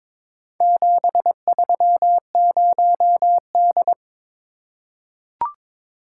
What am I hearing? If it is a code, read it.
Morse code, heard as a keyed tone, about 22 wpm, 700 Hz. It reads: 730D
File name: practice / 22wpm / Four Letter-Number Combinations - 22wpm